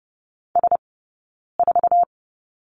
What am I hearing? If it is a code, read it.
Morse code, heard as a keyed tone, about 30 wpm, 700 Hz. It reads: S 4
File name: Lesson 19d - Letter w - Two Characters 30wpm